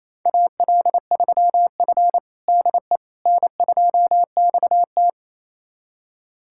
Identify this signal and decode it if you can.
Morse code, heard as a keyed tone, about 28 wpm, 700 Hz. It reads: AL3F DE N2XT
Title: USA Callsign DE Callsign - 4 Characters - 28wpm